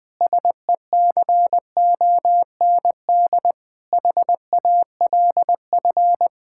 Morse code, heard as a keyed tone, about 20 wpm, 700 Hz. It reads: SECOND HALF
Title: Sentences with 5-Word Phrases - 20wpm